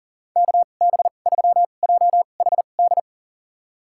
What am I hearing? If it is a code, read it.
Morse code, heard as a keyed tone, about 37 wpm, spaced as for 28 wpm, 700 Hz. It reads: KB3JHD
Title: USA Call Signs - 6 Characters - 2x Character Spacing - 40wpm